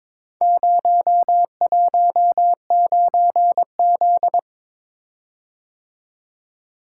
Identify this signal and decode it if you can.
Morse code, heard as a keyed tone, about 22 wpm, 700 Hz. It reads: 019Z
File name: practice / 22wpm / Four Letter-Number Combinations - 22wpm